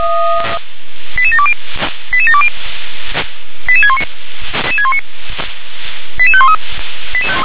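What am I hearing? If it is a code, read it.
Five-tone selective call (ZVEI), heard as quick runs of five tones, tones of about 70 ms. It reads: 88720 88720 88619 88513
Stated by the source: ZVEI five-tone call 88712 received by SDR.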